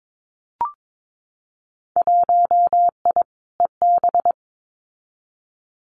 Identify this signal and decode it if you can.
Morse code, heard as a keyed tone, about 22 wpm, 700 Hz. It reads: 1I EB
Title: Winter Field Day Exchanges Encore 22wpm